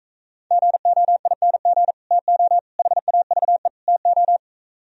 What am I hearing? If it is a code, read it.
Morse code, heard as a keyed tone, about 44 wpm, 700 Hz. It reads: GOING TO HAVE TO